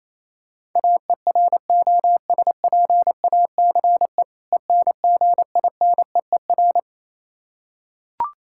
Morse code, heard as a keyed tone, about 27 wpm, 700 Hz. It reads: AEROSPACE ENGINEER
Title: Practice QSO - Exchange 3 - More Info - 28wpm